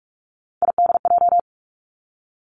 Morse code, heard as a keyed tone, about 46 wpm, 700 Hz. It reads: IDJ
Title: Three Challenging Letter-Number Combinations 45wpm